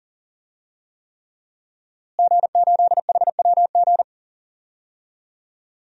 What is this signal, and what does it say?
Morse code, heard as a keyed tone, about 37 wpm, 700 Hz. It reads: G8HWG
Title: DXCC-100 Call Signs - 5 Characters - Repeated 3x - 40wpm